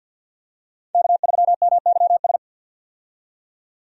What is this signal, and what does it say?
Morse code, heard as a keyed tone, about 51 wpm, 700 Hz. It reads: K3MYS
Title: USA Call Signs - 5 Characters - 50wpm